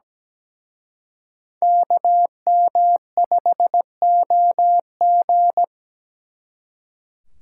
Morse code, heard as a keyed tone, about 20 wpm, 700 Hz. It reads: KM5OG
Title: Speed Racing - USA 2x2 Call Signs 17wpm